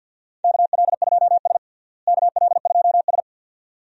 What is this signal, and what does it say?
Morse code, heard as a keyed tone, about 49 wpm, 700 Hz. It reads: KL2S KL2S
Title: USA Call Signs - 4 Characters - Repeated 2x - 50wpm